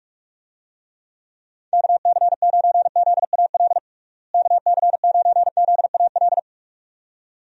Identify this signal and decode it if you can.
Morse code, heard as a keyed tone, about 46 wpm, 700 Hz. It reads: KC9ZAL KC9ZAL
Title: USA Call Signs - 6 Characters - Repeated 2x - 45wpm